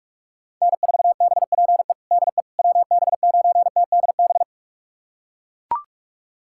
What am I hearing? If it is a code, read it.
Morse code, heard as a keyed tone, about 42 wpm, 700 Hz. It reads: N4BPE DE WB9TDB
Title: USA Callsign DE Callsign 45wpm